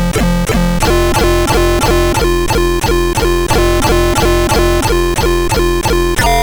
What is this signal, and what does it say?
There is no code.